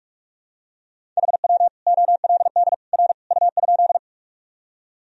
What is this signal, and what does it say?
Morse code, heard as a keyed tone, about 46 wpm, 700 Hz. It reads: HW OLD R U?
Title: Sentences from Top 200 Words and Abbreviations 45wpm